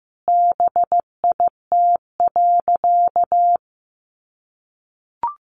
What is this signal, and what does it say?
Morse code, heard as a keyed tone, about 15 wpm, 700 Hz. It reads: BIT.
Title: Sentences from Top 900 Words 15wpm